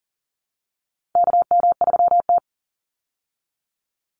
Morse code, heard as a keyed tone, about 40 wpm, 700 Hz. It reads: KM3T
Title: Call Signs Repeated 2x - 40wpm